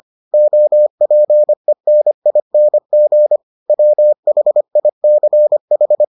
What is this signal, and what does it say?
Morse code, heard as a keyed tone, about 25 wpm, 600 Hz. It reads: OPENING WHICH